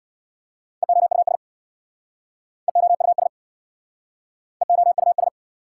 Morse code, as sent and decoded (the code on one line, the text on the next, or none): . --- ..- ... / . --- ..- ... / . --- ..- ...
EOUS EOUS EOUS